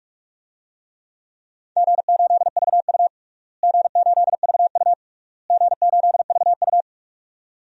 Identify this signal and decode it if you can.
Morse code, heard as a keyed tone, about 46 wpm, 700 Hz. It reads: G8VU G8VU G8VU
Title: DXCC-100 Call Signs - 4 Characters - Repeated 3x - 45wpm